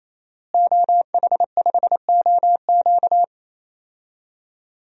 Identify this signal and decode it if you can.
Morse code, heard as a keyed tone, about 27 wpm, 700 Hz. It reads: OH5OQ
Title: DXCC-100 Call Signs - 5 Characters - Repeated 3x - 28wpm